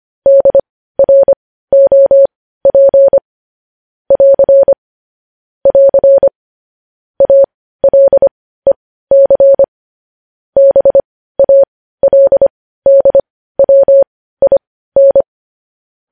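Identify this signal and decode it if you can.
Morse code, heard as a keyed tone, about 25 wpm, spaced as for 15 wpm, 550 Hz. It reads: DROP <AR> <AR> ALEC BALDWIN
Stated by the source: Morse code news-headline practice sent at 15 words per minute.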